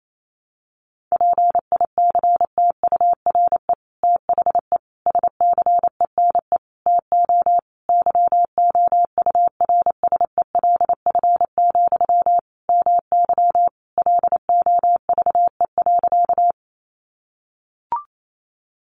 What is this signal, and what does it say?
Morse code, heard as a keyed tone, about 28 wpm, 700 Hz. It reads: PICTURE THE SCENE TO YOURSELF, MY LOVE.
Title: Sentences from Top 1100 Words 28wpm